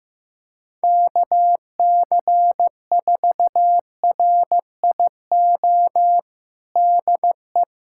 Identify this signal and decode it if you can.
Morse code, heard as a keyed tone, about 15 wpm, 700 Hz. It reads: KC4RIO DE
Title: USA Callsign DE Callsign 15wpm